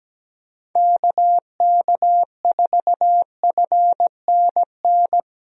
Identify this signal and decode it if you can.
Morse code, heard as a keyed tone, about 17 wpm, 700 Hz. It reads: KK4FNN
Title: USA Call Signs - 6 Characters - Repeated 2x - 17wpm